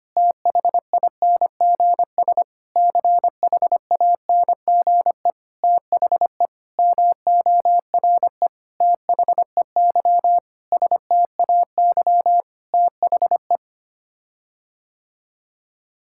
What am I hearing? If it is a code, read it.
Morse code, heard as a keyed tone, about 25 wpm, 700 Hz. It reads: THINGS CHANGE THE MORE THEY STAY THE
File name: Sets of 9 Words - 25wpm